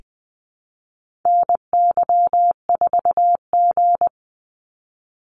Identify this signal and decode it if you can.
Morse code, heard as a keyed tone, about 20 wpm, 700 Hz. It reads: NY4G
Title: NA SOTA Activator Callsigns 20wpm